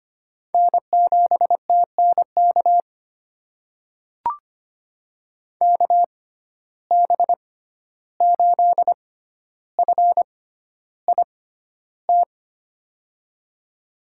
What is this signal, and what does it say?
Morse code, heard as a keyed tone, about 25 wpm, spaced as for 12 wpm, 700 Hz. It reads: N7TNK KB8FIT
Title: USA Call Signs - 6x Character Spacing - 25wpm